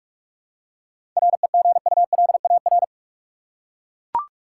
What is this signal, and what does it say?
Morse code, heard as a keyed tone, about 41 wpm, 700 Hz. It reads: REGULAR
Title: Seven Letter Words - 45wpm